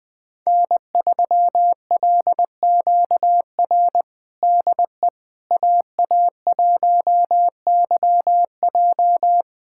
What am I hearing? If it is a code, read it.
Morse code, heard as a keyed tone, about 20 wpm, 700 Hz. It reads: N3LQR DE AA1YJ
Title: USA Callsign DE Callsign - 5 Characters - 20wpm